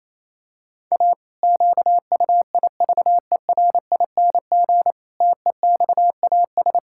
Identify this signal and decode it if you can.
Morse code, heard as a keyed tone, about 28 wpm, 700 Hz. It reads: A QUIVERING TEXAS
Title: Pangram Challenge v2 - 28wpm